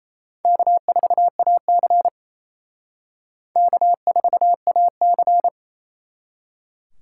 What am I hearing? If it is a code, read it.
Morse code, heard as a keyed tone, about 34 wpm, 700 Hz. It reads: K4AC K4AC
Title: Speed Racing - USA 1x2 Call Signs 28wpm